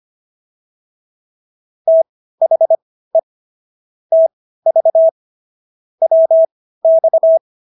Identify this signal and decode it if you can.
Morse code, heard as a keyed tone, about 25 wpm, spaced as for 15 wpm, 650 Hz. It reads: THE TV WX